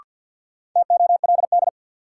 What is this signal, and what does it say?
Morse code, heard as a keyed tone, about 46 wpm, 700 Hz. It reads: TOLD
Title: Sentences from Top 400 Words - New Word Drill - 50wpm